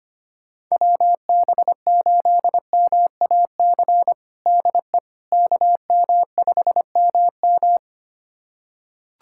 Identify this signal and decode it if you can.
Morse code, heard as a keyed tone, about 25 wpm, 700 Hz. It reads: WB8MAC DE KM5MM
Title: USA Callsign DE Callsign 25wpm